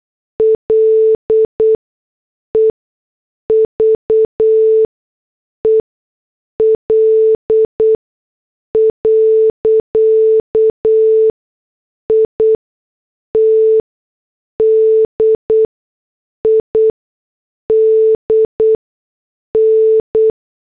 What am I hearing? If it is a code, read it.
Morse code, heard as a keyed tone, about 8 wpm, spaced as for 6 wpm, 440 Hz. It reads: LEVEL. IT DIDN
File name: morse / headlines